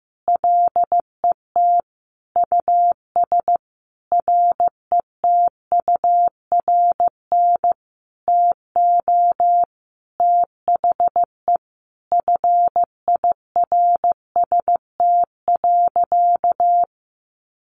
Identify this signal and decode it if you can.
Morse code, heard as a keyed tone, about 15 wpm, 700 Hz. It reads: LET US RETURN TO THE FIRST.
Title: Sentences from Top 800 Words 15wpm